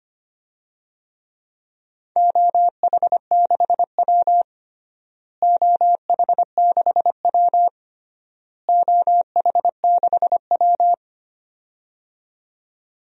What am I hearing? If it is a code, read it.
Morse code, heard as a keyed tone, about 25 wpm, 700 Hz. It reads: OH6W OH6W OH6W
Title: DXCC-100 Call Signs - 4 Characters - Repeated 3x - 25wpm